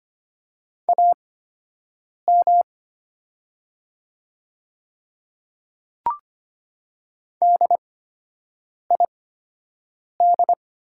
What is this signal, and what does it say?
Morse code, heard as a keyed tone, about 25 wpm, spaced as for 7 wpm, 700 Hz. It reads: AM DID
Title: 8x Character Spacing - Top 500 Words - 25wpm